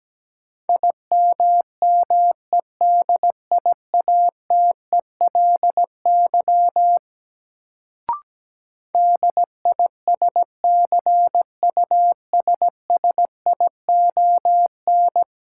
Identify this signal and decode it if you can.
Morse code, heard as a keyed tone, about 17 wpm, 700 Hz. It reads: IMMEDIATELY DISCUSSION
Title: Sentences from Top 1100 Words - New Word Drill - 17wpm